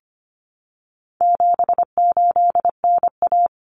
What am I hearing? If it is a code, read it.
Morse code, heard as a keyed tone, about 25 wpm, 700 Hz. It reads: 78NA
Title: Common Cut Numbers - 4-Digit 25wpm